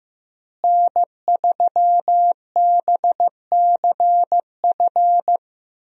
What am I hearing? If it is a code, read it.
Morse code, heard as a keyed tone, about 15 wpm, 700 Hz. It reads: N3BCF
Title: USA Call Signs - 5 Characters - Repeated 3x - 15wpm